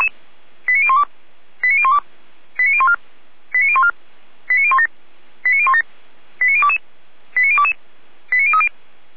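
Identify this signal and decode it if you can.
Five-tone selective call (ZVEI), heard as quick runs of five tones, tones of about 70 ms. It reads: 89012 89012 89014 89014 89017 89017 89022 89022 89030
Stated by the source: ZVEI five-tone selective call received via SDR.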